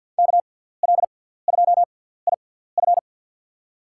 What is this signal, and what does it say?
Morse code, heard as a keyed tone, about 51 wpm, spaced as for 17 wpm, 700 Hz. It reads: KL2IF
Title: USA Call Signs - 5 Characters - 6x Character Spacing - 50wpm